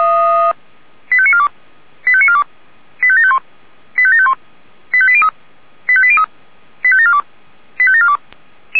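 Five-tone selective call (ZVEI), heard as quick runs of five tones, tones of about 70 ms. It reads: 85942 85942 85751 85751 85803 85803 85742 85742
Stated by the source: ZVEI five-tone selective call received by SDR.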